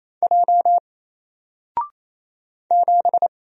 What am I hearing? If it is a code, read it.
Morse code, heard as a keyed tone, about 27 wpm, 700 Hz. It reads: J 7
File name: Single-Letter-Number-28wpm